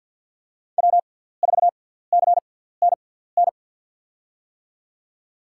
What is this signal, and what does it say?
Morse code, heard as a keyed tone, about 47 wpm, spaced as for 17 wpm, 700 Hz. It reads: W4CNN
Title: USA Call Signs - 6x Character Spacing - 50wpm